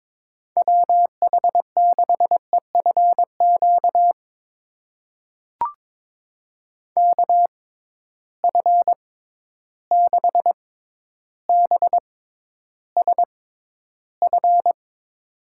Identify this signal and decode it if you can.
Morse code, heard as a keyed tone, about 22 wpm, spaced as for 11 wpm, 700 Hz. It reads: WH6EFQ KF6BSF
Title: USA Call Signs - 6 Characters - 6x Character Spacing - 22wpm